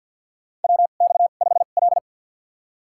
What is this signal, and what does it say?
Morse code, heard as a keyed tone, about 48 wpm, spaced as for 35 wpm, 700 Hz. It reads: WX5L
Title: USA Call Signs - 4 Characters - 2x Character Spacing - 50wpm